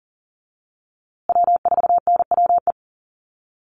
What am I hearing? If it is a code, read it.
Morse code, heard as a keyed tone, about 40 wpm, 700 Hz. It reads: W4NWE
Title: USA Call Signs Repeated 2x 40wpm